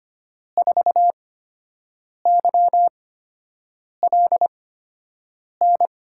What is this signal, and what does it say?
Morse code, heard as a keyed tone, about 25 wpm, spaced as for 7 wpm, 700 Hz. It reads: 4YLN
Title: USA Call Signs - 8x Character Spacing - 25wpm